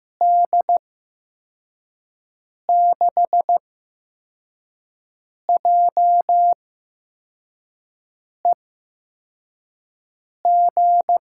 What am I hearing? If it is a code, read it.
Morse code, heard as a keyed tone, about 15 wpm, spaced as for 4 wpm, 700 Hz. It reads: D6JEG
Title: USA Call Signs - 8x Character Spacing - 15wpm